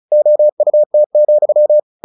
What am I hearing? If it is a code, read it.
Morse code, heard as a keyed tone, about 34 wpm, 600 Hz. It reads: OUT,
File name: the old man and the sea 2 35wpm